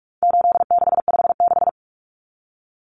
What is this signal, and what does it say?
Morse code, heard as a keyed tone, about 44 wpm, 700 Hz. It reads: 8656
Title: Common Cut Numbers - 4-Digit 45wpm